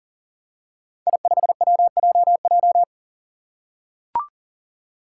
Morse code, heard as a keyed tone, about 40 wpm, 700 Hz. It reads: I5WJJ
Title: Five Challenging Letter-Number Combinations 40wpm